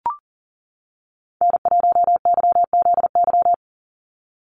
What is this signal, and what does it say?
Morse code, heard as a keyed tone, about 38 wpm, 690 Hz. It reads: N1YZY
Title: USA Call Signs 40wpm